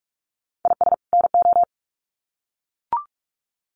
Morse code, heard as a keyed tone, about 44 wpm, 700 Hz. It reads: IS NO
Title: Sets of 2 Words 45wpm